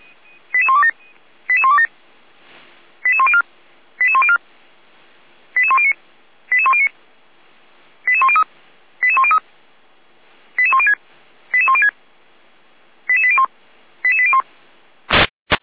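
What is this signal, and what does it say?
Five-tone selective call (ZVEI), heard as quick runs of five tones, tones of about 70 ms. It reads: 88127 88127 88104 88104 88109 88109 88103 88103 88106 88106 88801 88801
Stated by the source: Five-tone ZVEI selective call 88102, received on an SDR by the Orlamünde fire brigade (feuerwehr-orlamuende).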